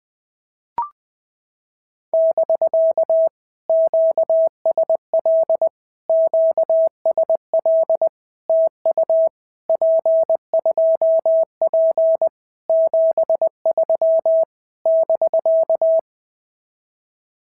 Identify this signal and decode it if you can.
Morse code, heard as a keyed tone, about 20 wpm, 650 Hz. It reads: <BK> QSL QSL TU P2P 73 <BK>
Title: POTA QSO Practice - 20wpm